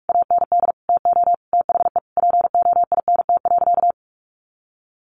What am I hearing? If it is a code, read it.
Morse code, heard as a keyed tone, about 44 wpm, 700 Hz. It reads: AND TO THE POINT.
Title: Sentences from Top 1400 Words 45wpm